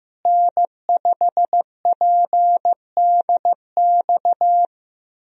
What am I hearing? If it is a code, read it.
Morse code, heard as a keyed tone, about 15 wpm, 700 Hz. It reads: N5PDX
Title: USA Call Signs - 5 Characters - Repeated 3x - 15wpm